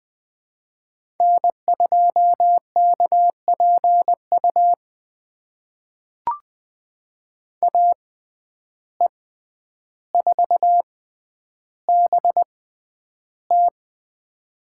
Morse code, heard as a keyed tone, about 20 wpm, spaced as for 10 wpm, 700 Hz. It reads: N2KPU AE4BT
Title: USA Call Signs - 5 Characters - 6x Character Spacing - 20wpm